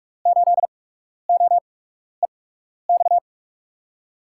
Morse code, heard as a keyed tone, about 45 wpm, spaced as for 12 wpm, 700 Hz. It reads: 8OEX
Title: USA Call Signs - 6 Characters - 8x Character Spacing - 45wpm